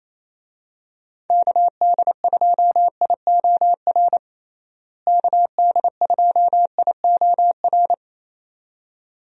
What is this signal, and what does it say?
Morse code, heard as a keyed tone, about 28 wpm, 700 Hz. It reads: KD2IOR KD2IOR
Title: USA Call Signs - 6 Characters - Repeated 2x - 28wpm